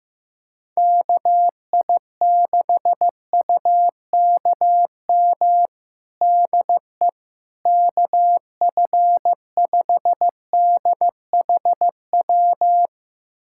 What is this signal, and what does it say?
Morse code, heard as a keyed tone, about 15 wpm, 700 Hz. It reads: KI6UKM DE KF5DHW
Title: USA Callsign DE Callsign - 6 Characters - 15wpm